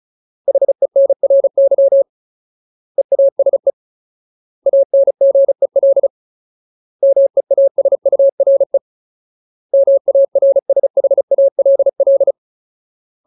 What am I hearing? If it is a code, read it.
Morse code, heard as a keyed tone, about 34 wpm, 550 Hz. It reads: HENRY EASE ANGEL MEASURE MARSHALL
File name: Ultimate Word List / 35WPM Ultimate